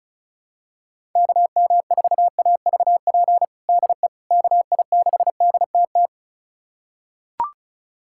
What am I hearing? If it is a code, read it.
Morse code, heard as a keyed tone, about 34 wpm, 700 Hz. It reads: KM4AVP DE KI6DTT
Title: USA Callsign DE Callsign - 6 Characters - 35wpm